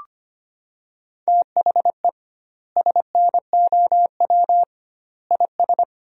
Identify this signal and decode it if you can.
Morse code, heard as a keyed tone, about 25 wpm, 700 Hz. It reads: THE SNOW IS
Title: 2x Word-Spacing - Sentences from Top 500 Words - 25wpm